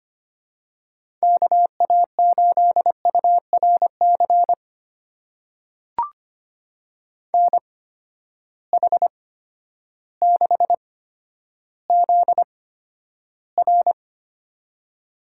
Morse code, heard as a keyed tone, about 25 wpm, spaced as for 11 wpm, 700 Hz. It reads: KA8URC NH6ZR
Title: USA Call Signs - 8x Character Spacing - 25wpm